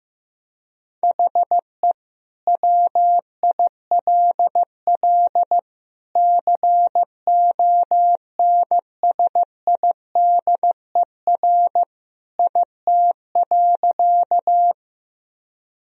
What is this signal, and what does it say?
Morse code, heard as a keyed tone, about 15 wpm, 700 Hz. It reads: HE WILL CONSIDER IT.
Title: Sentences from Top 1000 Words - Review Part 4 - 15wpm